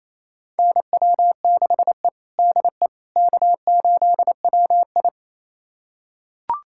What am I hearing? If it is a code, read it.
Morse code, heard as a keyed tone, about 28 wpm, 700 Hz. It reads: NW6E DE K8WI
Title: USA Callsign DE Callsign - 4 Characters - 28wpm